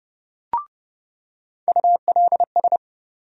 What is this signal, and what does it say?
Morse code, heard as a keyed tone, about 30 wpm, 700 Hz. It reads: ULS
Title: USA Call Sign Postfixes - 3 Letters 30wpm